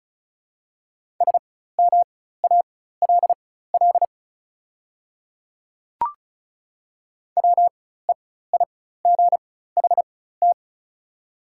Morse code, heard as a keyed tone, about 36 wpm, spaced as for 16 wpm, 700 Hz. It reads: SMALL WEIGHT
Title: 4x Character Spacing - Top 500 Words - 35wpm